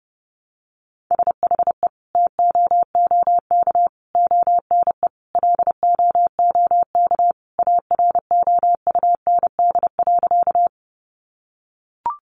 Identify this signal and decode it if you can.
Morse code, heard as a keyed tone, about 30 wpm, 700 Hz. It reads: SHE TOOK ONE LOOK AROUND.
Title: Sentences from Top 600 Words 30wpm